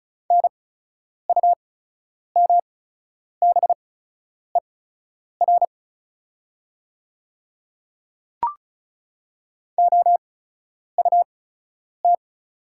Morse code, heard as a keyed tone, about 34 wpm, spaced as for 10 wpm, 690 Hz. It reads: NUMBER OUT
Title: 8x Character Spacing - Top 100 Words - 35wpm